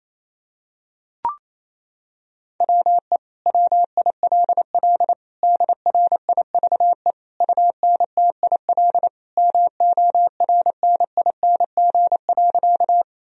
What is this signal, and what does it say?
Morse code, heard as a keyed tone, about 28 wpm, 700 Hz. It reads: WE WILL DRIVE UNTIL MORNING.
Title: Sentences from Top 1000 Words - Review Part 2 - 28wpm